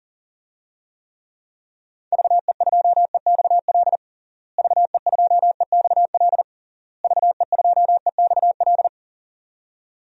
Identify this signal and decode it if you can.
Morse code, heard as a keyed tone, about 37 wpm, 700 Hz. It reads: VE2EXL VE2EXL VE2EXL
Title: DXCC-100 Call Signs - 6 Characters - Repeated 3x - 40wpm